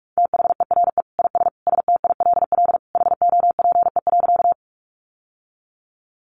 Morse code, heard as a keyed tone, about 44 wpm, 700 Hz. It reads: THERE IS STILL HOPE.
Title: Sentences from Top 1000 Words 45wpm